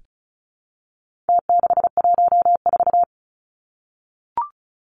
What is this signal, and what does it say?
Morse code, heard as a keyed tone, about 34 wpm, 700 Hz. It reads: T614
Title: Numbers - Contest Style - 4-Digit 35wpm